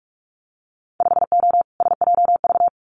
Morse code, heard as a keyed tone, about 44 wpm, 700 Hz. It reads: 5O SJV
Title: Winter Field Day Exchanges Encore 45wpm